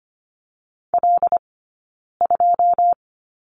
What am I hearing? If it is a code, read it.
Morse code, heard as a keyed tone, about 25 wpm, spaced as for 9 wpm, 690 Hz. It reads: L2
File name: course / 25:10wpm / 092 Lesson 16d - Letter c - Two Characters 25wpm